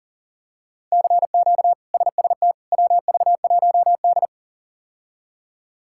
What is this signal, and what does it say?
Morse code, heard as a keyed tone, about 40 wpm, 700 Hz. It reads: CQ SST WV1D
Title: K1USN SST QSO Practice - 40wpm